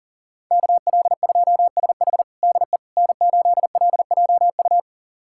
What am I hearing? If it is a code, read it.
Morse code, heard as a keyed tone, about 40 wpm, 700 Hz. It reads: KL2SH DE N8LJU
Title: USA Callsign DE Callsign 40wpm